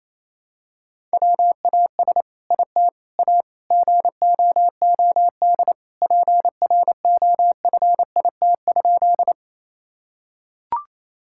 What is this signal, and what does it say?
Morse code, heard as a keyed tone, about 28 wpm, 700 Hz. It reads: WAS IT A GOOD PROFIT?
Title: Sentences from Top 1000 Words - Review Part 5 - 28wpm